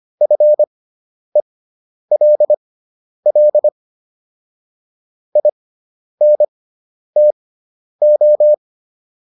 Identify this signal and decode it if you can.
Morse code, heard as a keyed tone, about 25 wpm, spaced as for 10 wpm, 600 Hz. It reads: FELL INTO